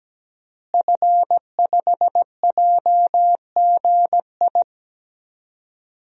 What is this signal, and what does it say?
Morse code, heard as a keyed tone, about 17 wpm, 700 Hz. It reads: F5JGI
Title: DXCC-100 Call Signs - 5 Characters - Repeated 2x - 17wpm